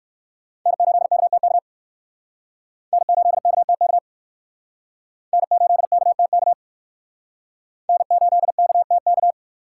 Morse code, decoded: N8KTK N8KTK N8KTK N8KTK

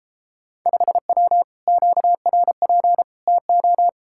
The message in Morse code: 5W QRP TO

